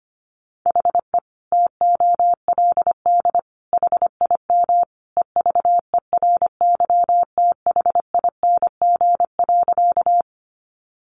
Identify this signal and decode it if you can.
Morse code, heard as a keyed tone, about 25 wpm, 700 Hz. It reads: HE TOLD HIM EVERYTHING.
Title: Sentences from Top 700 Words 25wpm